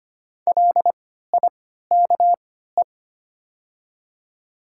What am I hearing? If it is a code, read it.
Morse code, heard as a keyed tone, about 25 wpm, spaced as for 14 wpm, 700 Hz. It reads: LIKE
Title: 3x Character Spacing - Top 100 Words - 25wpm